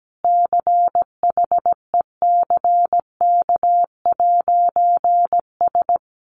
Morse code, heard as a keyed tone, about 17 wpm, 700 Hz. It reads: CHECK'S
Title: A Sound Of Thunder 17wpm